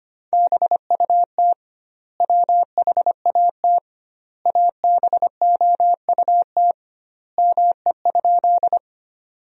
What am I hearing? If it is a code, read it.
Morse code, heard as a keyed tone, about 25 wpm, 700 Hz. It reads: BUT WHAT ABOUT ME?
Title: 2x Word-Spacing - Sentences from Top 200 Words - 25wpm